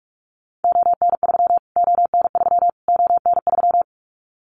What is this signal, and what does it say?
Morse code, heard as a keyed tone, about 44 wpm, 700 Hz. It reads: ON3 ON3 ON3
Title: Callsign Prefixes for Top 25 DXCC - Repeated 3x - 45wpm